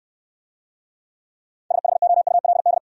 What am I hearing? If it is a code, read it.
Morse code, heard as a keyed tone, about 63 wpm, 700 Hz. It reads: SHOULD